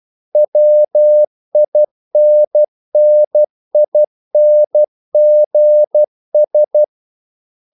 Morse code, heard as a keyed tone, about 12 wpm, 600 Hz. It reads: WINNINGS
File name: CW / Practice / If12